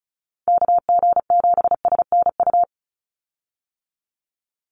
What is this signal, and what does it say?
Morse code, heard as a keyed tone, about 34 wpm, 700 Hz. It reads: KG7SNU
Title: USA Call Signs 35wpm